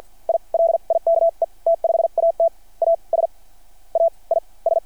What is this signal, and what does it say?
Morse code, heard as a keyed tone, about 53 wpm, spaced as for 38 wpm, 660 Hz. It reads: I LIKE T5AT AS A IS
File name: archived / CRSnet